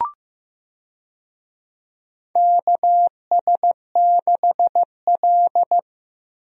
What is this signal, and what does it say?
Morse code, heard as a keyed tone, about 15 wpm, 700 Hz. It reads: KS6L
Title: USA Call Signs - 4 Characters - Repeated 2x - 15wpm